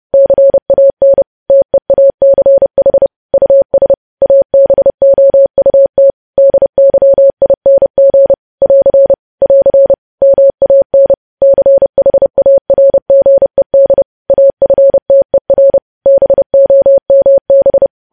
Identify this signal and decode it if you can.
Morse code, heard as a keyed tone, about 30 wpm, 550 Hz. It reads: CAN TEACH US ABOUT DYING <AR> <AR> MAN CHARGED AFTER BOMB